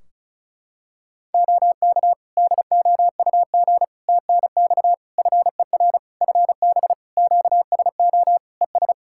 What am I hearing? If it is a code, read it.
Morse code, heard as a keyed tone, about 36 wpm, 700 Hz. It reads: OK DOUG TNX FER FB QSO ES
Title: K1USN SST QSO Practice 35wpm